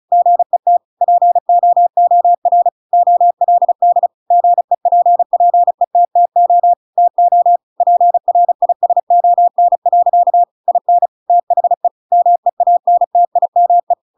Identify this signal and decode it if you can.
Morse code, heard as a keyed tone, about 34 wpm, 700 Hz. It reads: GET POOR OLD GEPPETTO TO PRISON. IN THE MEANTIME